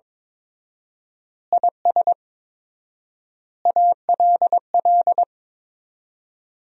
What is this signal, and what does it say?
Morse code, heard as a keyed tone, about 22 wpm, 700 Hz. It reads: IS ALL
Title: 4x Word-Spacing - Sentences from Top 200 Words - 22wpm